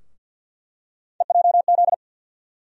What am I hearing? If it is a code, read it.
Morse code, heard as a keyed tone, about 52 wpm, 700 Hz. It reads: EJZ
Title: USA Call Sign Postfixes - 3 Letters 50wpm